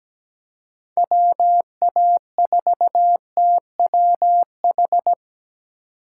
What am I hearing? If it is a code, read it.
Morse code, heard as a keyed tone, about 20 wpm, 700 Hz. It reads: WA4TWH
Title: Speed Racing - USA Call Signs 17wpm